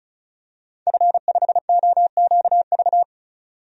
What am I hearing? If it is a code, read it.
Morse code, heard as a keyed tone, about 34 wpm, 700 Hz. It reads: F5OQV
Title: DXCC-100 Call Signs - 5 Characters - 35wpm